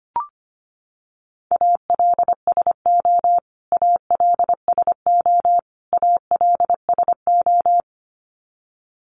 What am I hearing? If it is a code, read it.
Morse code, heard as a keyed tone, about 25 wpm, 700 Hz. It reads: ALSO ALSO ALSO
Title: Top 500 Words Repeated 25wpm